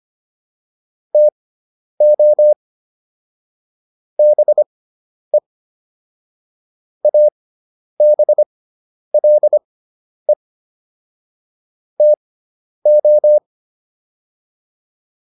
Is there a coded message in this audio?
Yes: Morse code from 1.1 to 13.4 s, heard as a keyed tone, about 25 wpm, spaced as for 10 wpm, 600 Hz. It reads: TO BE ABLE TO